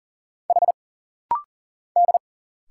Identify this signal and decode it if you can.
Morse code, heard as a keyed tone, about 40 wpm, 700 Hz. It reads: H D
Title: Single Letters - Rapid-Fire 40wpm